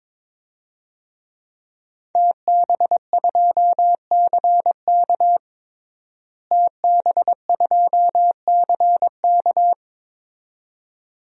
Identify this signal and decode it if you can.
Morse code, heard as a keyed tone, about 22 wpm, 700 Hz. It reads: TB2CK TB2CK
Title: DXCC-100 Call Signs - 5 Characters - Repeated 2x - 22wpm